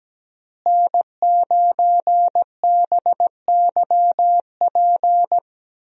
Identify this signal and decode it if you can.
Morse code, heard as a keyed tone, about 17 wpm, 700 Hz. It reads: N9BYP
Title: USA Call Signs - 5 Characters - Repeated 2x - 17wpm